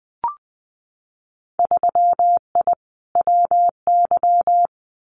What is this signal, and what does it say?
Morse code, heard as a keyed tone, about 20 wpm, 700 Hz. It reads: 3I WY
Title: Winter Field Day Exchanges Encore 20wpm